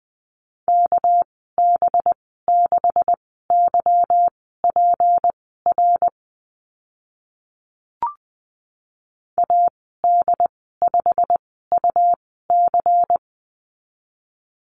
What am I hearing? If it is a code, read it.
Morse code, heard as a keyed tone, about 20 wpm, spaced as for 15 wpm, 700 Hz. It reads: KB6YPR AD5UC
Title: 238 Lesson 38g - Letter q - Call Signs 20wpm